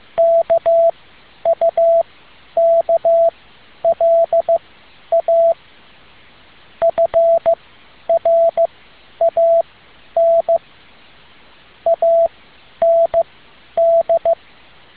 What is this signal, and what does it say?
Morse code, heard as a keyed tone, about 15 wpm, spaced as for 10 wpm, 670 Hz. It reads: KUKLA FRAN AND